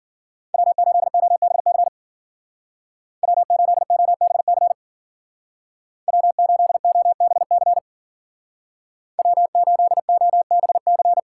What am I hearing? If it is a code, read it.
Morse code, heard as a keyed tone, about 51 wpm, 700 Hz. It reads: W8OBC W8OBC W8OBC W8OBC